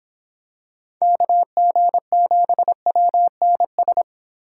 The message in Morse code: KG7WNS

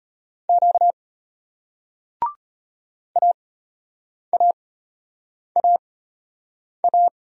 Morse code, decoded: Q AAAA

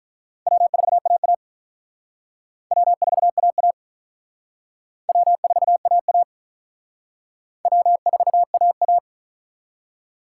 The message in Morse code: W4AA W4AA W4AA W4AA

